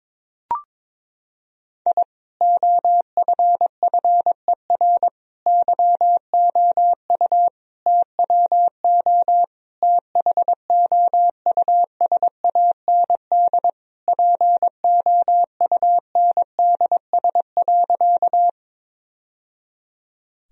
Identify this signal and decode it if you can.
Morse code, heard as a keyed tone, about 22 wpm, 700 Hz. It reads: I OFFER YOU TWO THOUSAND POUNDS.
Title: Sentences from Top 1000 Words - Review Part 4 - 22wpm